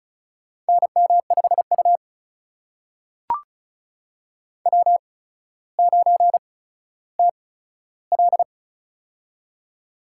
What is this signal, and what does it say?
Morse code, heard as a keyed tone, about 36 wpm, spaced as for 15 wpm, 700 Hz. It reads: NM5U W9TL
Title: USA Call Signs - 4 Characters - 8x Character Spacing - 35wpm